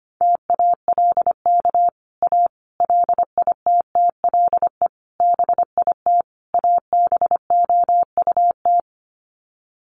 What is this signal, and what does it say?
Morse code, heard as a keyed tone, about 25 wpm, 700 Hz. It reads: TALK A LITTLE BIT ABOUT
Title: Sets of 5 Words Encore 25wpm